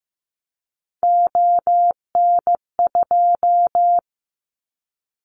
Morse code, heard as a keyed tone, about 15 wpm, 700 Hz. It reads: ON2
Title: Callsign Prefixes for Top 50 DXCC - 15wpm